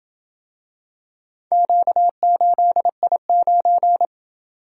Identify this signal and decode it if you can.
Morse code, heard as a keyed tone, about 27 wpm, 700 Hz. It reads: Q8I9